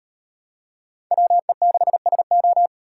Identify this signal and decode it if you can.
Morse code, heard as a keyed tone, about 38 wpm, 700 Hz. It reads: WE6SO